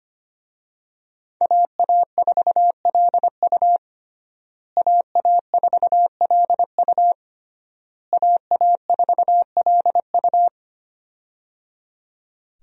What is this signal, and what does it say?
Morse code, heard as a keyed tone, about 25 wpm, 700 Hz. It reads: AA4LU AA4LU AA4LU
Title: USA Call Signs - 5 Characters - Repeated 3x - 25wpm